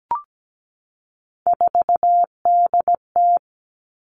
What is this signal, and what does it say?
Morse code, heard as a keyed tone, about 17 wpm, 700 Hz. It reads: 4DT
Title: All Cut Numbers - 3-Digit 17wpm